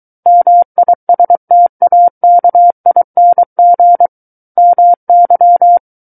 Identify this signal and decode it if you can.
Morse code, heard as a keyed tone, about 23 wpm, 700 Hz. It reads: MISTAKING MY
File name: Book One / 14-Chapter 14 - 23 WPM